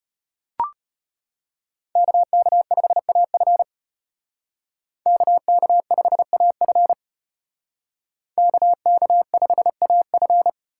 Morse code, heard as a keyed tone, about 32 wpm, 700 Hz. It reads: KK5AF KK5AF KK5AF